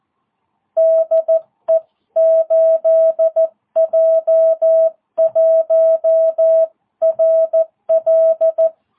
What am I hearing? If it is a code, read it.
Morse code, heard as a keyed tone, about 14 wpm, 660 Hz. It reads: DE 8J1RL